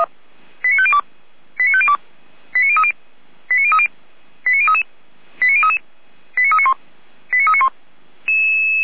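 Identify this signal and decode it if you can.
Five-tone selective call (ZVEI), heard as quick runs of five tones, tones of about 70 ms. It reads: 89502 89502 89030 89030 89033 89033 89391 89391
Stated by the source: Received on SDR, ZVEI five-tone call.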